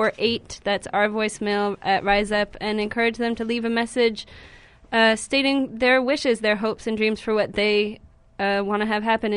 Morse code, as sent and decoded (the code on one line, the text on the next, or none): none